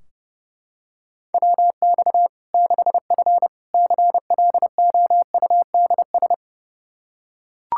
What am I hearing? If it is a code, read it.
Morse code, heard as a keyed tone, about 30 wpm, 700 Hz. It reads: WX 6F CLOUDS
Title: Practice QSO - Exchange 2 - Info - 30wpm